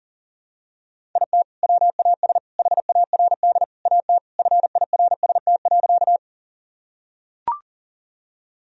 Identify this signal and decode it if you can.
Morse code, heard as a keyed tone, about 37 wpm, 700 Hz. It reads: IT WAS HARD AT FIRST.
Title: Sentences from Top 1000 Words - Review Part 1 - 40wpm